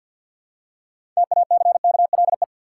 Morse code, heard as a keyed tone, about 51 wpm, 700 Hz. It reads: TACKLE
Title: Six Letter Words - 50wpm